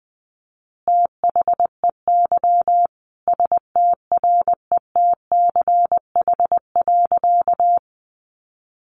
Morse code, heard as a keyed tone, about 20 wpm, 700 Hz. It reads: THEY STRETCH.
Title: Easy Phrases 20wpm